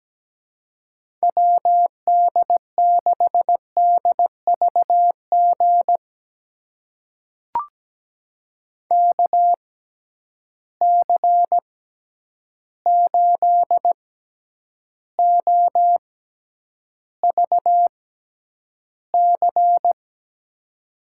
Morse code, heard as a keyed tone, about 17 wpm, spaced as for 9 wpm, 700 Hz. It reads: WD6DVG KC8OVC
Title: USA Call Signs - 6 Characters - 6x Character Spacing - 17wpm